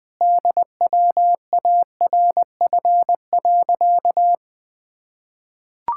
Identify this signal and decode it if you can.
Morse code, heard as a keyed tone, about 20 wpm, 700 Hz. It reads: DWARF.
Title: Pangram Challenge v2 - 20wpm